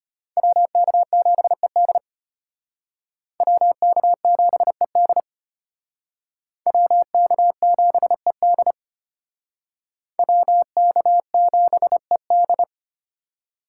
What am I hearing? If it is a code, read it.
Morse code, heard as a keyed tone, about 33 wpm, 700 Hz. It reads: WK7ED WK7ED WK7ED WK7ED